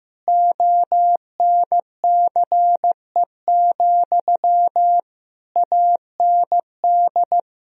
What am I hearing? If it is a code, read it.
Morse code, heard as a keyed tone, about 15 wpm, 700 Hz. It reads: ONCE, AND
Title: Sentences from Top 1000 Words - Review Part 1 - 15wpm